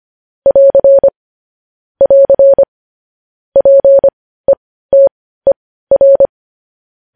Morse code, heard as a keyed tone, about 25 wpm, spaced as for 15 wpm, 550 Hz. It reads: <AR> <AR> PETER
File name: podcast / News headlines-1x-25x15